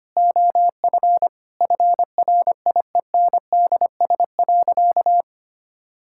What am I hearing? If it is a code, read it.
Morse code, heard as a keyed tone, about 25 wpm, 700 Hz. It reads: OF FRIENDS.
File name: practice / 25wpm / Sentences with 4-Word Phrases - 25wpm